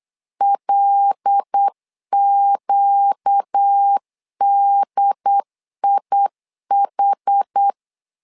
Morse code, heard as a keyed tone, about 8 wpm, 800 Hz. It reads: LQDIH
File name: LEZIONI / 5 LBVC / 5 LBV 8wpm 2837